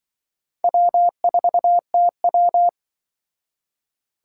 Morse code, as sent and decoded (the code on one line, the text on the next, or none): .-- ....- - .--
W4TW